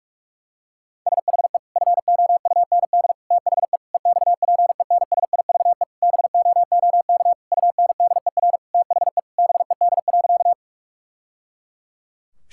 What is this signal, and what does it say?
Morse code, heard as a keyed tone, about 46 wpm, 700 Hz. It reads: SHE FOUND THE EXPENSIVE BOOK UNDER THE BED.
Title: Sentences from Top 1000 Words - Review Part 6 - 45wpm